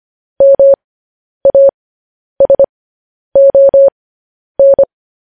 Morse code, heard as a keyed tone, about 25 wpm, spaced as for 10 wpm, 550 Hz. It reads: MASON
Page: News Headlines in Morse Code at 10 WPM